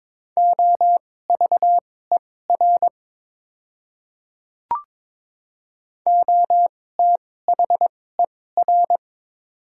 Morse code, heard as a keyed tone, about 22 wpm, spaced as for 16 wpm, 700 Hz. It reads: OVER OTHER
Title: 2x Character Spacing - Top 200 Words - 22wpm